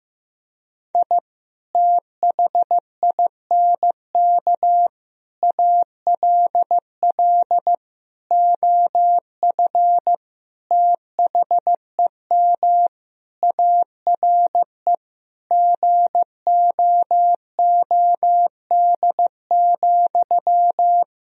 Morse code, heard as a keyed tone, about 15 wpm, 700 Hz. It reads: I THINK ALL OF THEM ARE GOOD,
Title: Sentences from Top 1000 Words - Review Part 4 - 15wpm